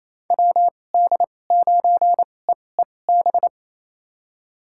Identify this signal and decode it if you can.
Morse code, heard as a keyed tone, about 28 wpm, spaced as for 20 wpm, 700 Hz. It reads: WD9EEB
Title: USA Call Signs - 6 Characters - 2x Character Spacing - 28wpm